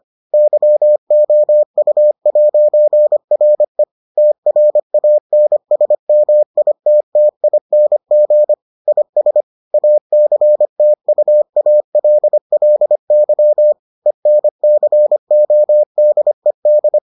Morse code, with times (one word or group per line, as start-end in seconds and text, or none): 0.3-3.8 YOU'RE
4.2-8.5 TRANSMITTING
8.9-9.4 IS
9.7-13.7 ACTUALLY
14.1-17.0 ENCODED